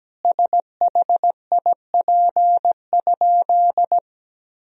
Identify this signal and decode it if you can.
Morse code, heard as a keyed tone, about 17 wpm, 700 Hz. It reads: SHIP?
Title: Sentences from Top 400 Words and Abbreviations 17wpm